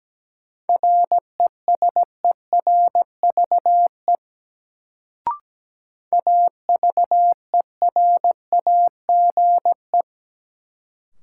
Morse code, heard as a keyed tone, about 17 wpm, 700 Hz. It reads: RESERVE AVERAGE
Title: Sentences from Top 1000 Words - Review Part 6 - New Word Drill - 17wpm